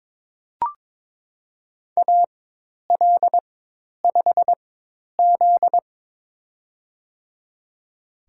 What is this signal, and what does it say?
Morse code, heard as a keyed tone, about 23 wpm, spaced as for 10 wpm, 700 Hz. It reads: AL5Z
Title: USA Call Signs - 4 Characters - 4x Character Spacing - 22wpm